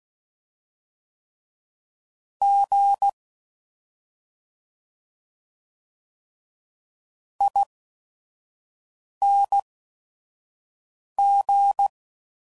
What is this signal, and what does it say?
Morse code, heard as a keyed tone, about 16 wpm, spaced as for 5 wpm, 800 Hz. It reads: G ING
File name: Suffixes / Special ING's / Sngl ING's